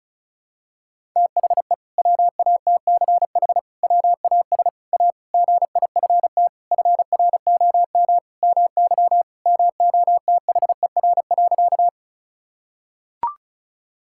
Morse code, heard as a keyed tone, about 34 wpm, 700 Hz. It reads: THE WATCH WAS A GIFT FROM MY MOTHER.
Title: Sentences from Top 1000 Words - Review Part 6 - 35wpm